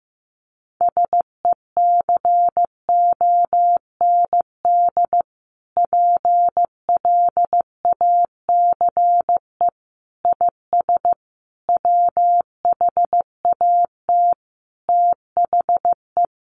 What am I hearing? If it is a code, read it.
Morse code, heard as a keyed tone, about 15 wpm, 700 Hz. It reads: SECOND PLACE IS WHAT THE
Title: Sentences from Top 1500 Words 15wpm